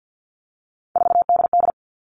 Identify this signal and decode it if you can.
Morse code, heard as a keyed tone, about 50 wpm, 700 Hz. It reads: 4DD
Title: All Cut Numbers - 3-Digit 50wpm